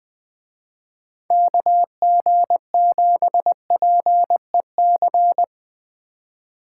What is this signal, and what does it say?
Morse code, heard as a keyed tone, about 23 wpm, 700 Hz. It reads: KG7PEC
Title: Speed Racing - USA Call Signs 20wpm